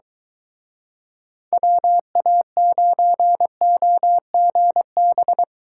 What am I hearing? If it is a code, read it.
Morse code, heard as a keyed tone, about 23 wpm, 700 Hz. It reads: WA9OGB